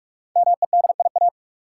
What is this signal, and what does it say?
Morse code, heard as a keyed tone, about 48 wpm, 700 Hz. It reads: MEDIA
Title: Five Letter Words - 45wpm